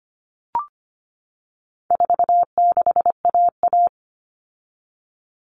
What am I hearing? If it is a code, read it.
Morse code, heard as a keyed tone, about 25 wpm, 700 Hz. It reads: 46AA
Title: All Cut Numbers - 4-Digit 25wpm